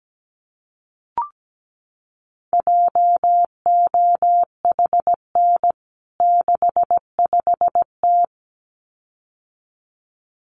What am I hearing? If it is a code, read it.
Morse code, heard as a keyed tone, about 17 wpm, 700 Hz. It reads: JOHN 65T
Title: CWT Exchange 17wpm